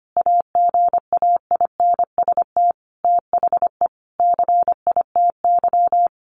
Morse code, heard as a keyed tone, about 25 wpm, 700 Hz. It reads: AGAINST THE CITY
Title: Sentences from Top 1200 Words 25wpm